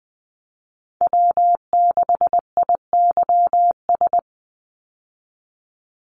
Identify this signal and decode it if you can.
Morse code, heard as a keyed tone, about 20 wpm, 700 Hz. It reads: W6IYS
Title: Call Signs Repeated 3x - 20wpm